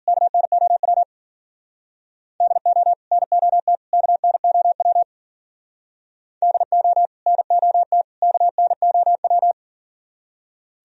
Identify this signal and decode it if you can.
Morse code, heard as a keyed tone, about 50 wpm, 700 Hz. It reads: KNOW DO NOT KNOW DO NOT KNOW